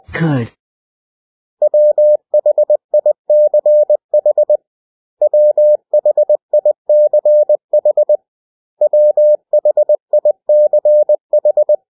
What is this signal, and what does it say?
Morse code, heard as a keyed tone, about 20 wpm, 600 Hz. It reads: WHICH WHICH WHICH
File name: cw / 100 Common English Words